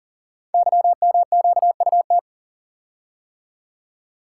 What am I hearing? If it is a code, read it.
Morse code, heard as a keyed tone, about 40 wpm, 690 Hz. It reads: YMQUT
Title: Code Groups - 5 Letters - 40wpm